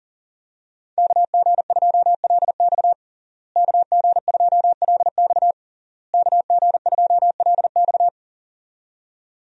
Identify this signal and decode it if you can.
Morse code, heard as a keyed tone, about 40 wpm, 700 Hz. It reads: KG2LX KG2LX KG2LX
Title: USA Call Signs - 5 Characters - Repeated 3x - 40wpm